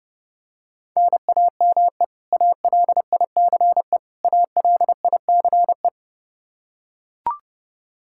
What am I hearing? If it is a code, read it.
Morse code, heard as a keyed tone, about 30 wpm, 700 Hz. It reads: NAME ALICE ALICE
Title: Practice QSO - Exchange 1 - Beginning - 30wpm